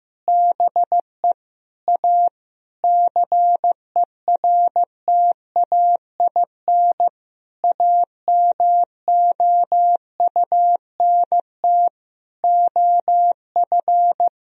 Morse code, heard as a keyed tone, about 14 wpm, 700 Hz. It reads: BE A CERTAIN AMOUNT OF
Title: Sentences from Top 1000 Words - Review Part 4 - 15wpm